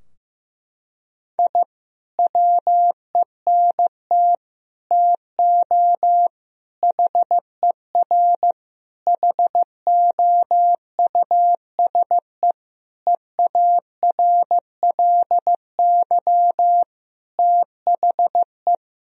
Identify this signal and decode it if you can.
Morse code, heard as a keyed tone, about 15 wpm, 700 Hz. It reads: I WENT TO HER HOUSE EARLY THE
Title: Sentences from Top 1000 Words - Review Part 2 - 15wpm